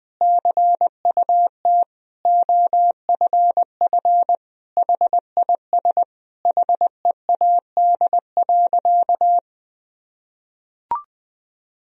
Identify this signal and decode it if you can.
Morse code, heard as a keyed tone, about 20 wpm, 700 Hz. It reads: CUT OFF HIS HEAD.
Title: Sentences from Top 1000 Words - Review Part 1 - 20wpm